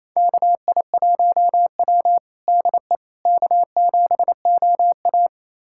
Morse code, heard as a keyed tone, about 28 wpm, 700 Hz. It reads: KI1W DE K7OA
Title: USA Callsign DE Callsign - 4 Characters - 28wpm